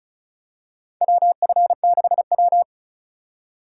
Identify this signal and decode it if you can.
Morse code, heard as a keyed tone, about 36 wpm, 700 Hz. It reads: WF6W
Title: USA Call Signs - 4 Characters - 3x Character Spacing - 35wpm